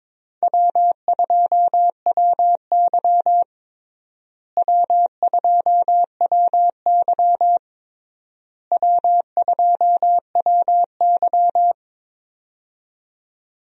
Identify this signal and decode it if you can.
Morse code, heard as a keyed tone, about 22 wpm, 700 Hz. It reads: W2WY W2WY W2WY
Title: USA Call Signs - 4 Characters - Repeated 3x - 22wpm